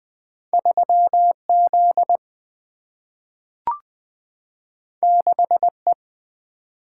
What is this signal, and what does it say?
Morse code, heard as a keyed tone, about 20 wpm, 700 Hz. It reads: 3Z 6E
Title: Single-Letter Single-Number 20wpm